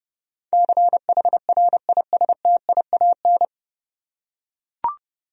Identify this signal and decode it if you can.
Morse code, heard as a keyed tone, about 30 wpm, 700 Hz. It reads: CHRISTIAN
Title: USA Common Names - 30wpm